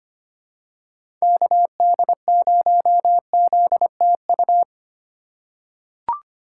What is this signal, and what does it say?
Morse code, heard as a keyed tone, about 25 wpm, 700 Hz. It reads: KD0ZTU
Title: USA Call Signs - 6 Characters - 3x Character Spacing - 25wpm